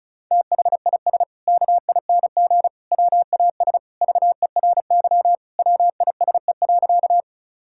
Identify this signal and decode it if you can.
Morse code, heard as a keyed tone, about 36 wpm, 700 Hz. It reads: THIS KING WAS VERY WISE.
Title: Sentences from Top 1600 Words 35wpm